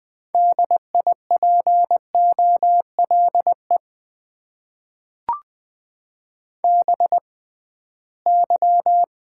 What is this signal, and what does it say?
Morse code, heard as a keyed tone, about 20 wpm, spaced as for 15 wpm, 700 Hz. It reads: DIPOLE BY
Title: Top 90 QSO Elements - 6x Character Spacing - 20wpm